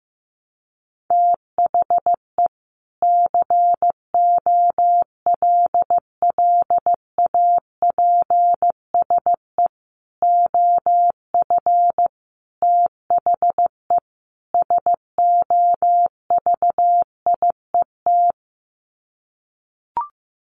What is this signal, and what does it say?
Morse code, heard as a keyed tone, about 15 wpm, 700 Hz. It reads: THE COLLAPSE OF THE SOVIET
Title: Sets of 5 Words Encore 15wpm